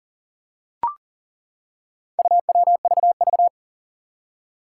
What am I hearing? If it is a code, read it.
Morse code, heard as a keyed tone, about 43 wpm, 700 Hz. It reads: UWVV
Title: Four Challenging Letter-Number Combinations 40wpm